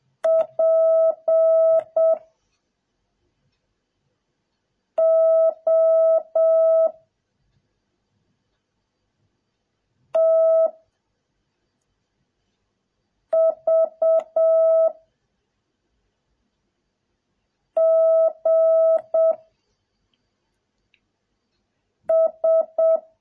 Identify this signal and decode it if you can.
Morse code, heard as a keyed tone, about 7 wpm, 650 Hz. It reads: P O T V G S